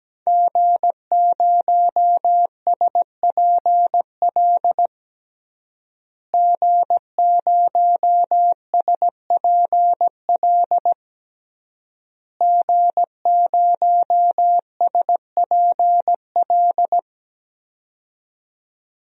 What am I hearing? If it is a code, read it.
Morse code, heard as a keyed tone, about 17 wpm, 700 Hz. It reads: G0SPL G0SPL G0SPL
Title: DXCC-100 Call Signs - 5 Characters - Repeated 3x - 17wpm